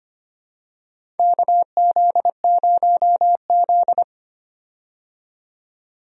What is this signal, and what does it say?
Morse code, heard as a keyed tone, about 25 wpm, 700 Hz. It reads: KZ0Z
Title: USA Call Signs - 4 Characters - 25wpm